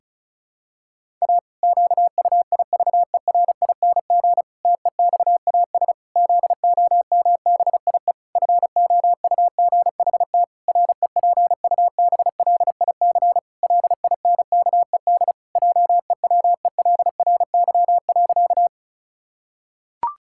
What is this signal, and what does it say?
Morse code, heard as a keyed tone, about 36 wpm, 700 Hz. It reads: A QUIVERING TEXAS ZOMBIE FOUGHT REPUBLIC LINKED JEWELRY.
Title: Pangram Challenge v2 - 35wpm